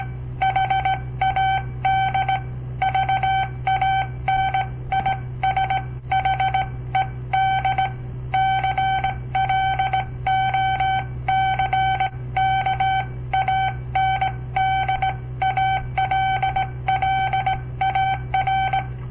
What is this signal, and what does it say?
Morse code, heard as a keyed tone, about 19 wpm, spaced as for 16 wpm, 770 Hz. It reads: HAD VANISHED CLOCK AND ALLAR
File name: papertapes / local / Code Practice-01